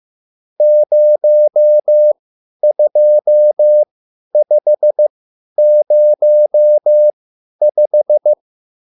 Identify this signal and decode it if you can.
Morse code, heard as a keyed tone, about 15 wpm, spaced as for 10 wpm, 600 Hz. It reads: 02505
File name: QSL / CW lesson